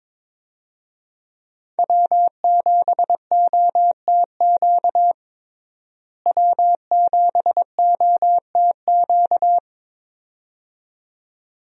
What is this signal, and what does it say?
Morse code, heard as a keyed tone, about 22 wpm, 700 Hz. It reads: W7OTQ W7OTQ
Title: USA Call Signs - 5 Characters - Repeated 2x - 22wpm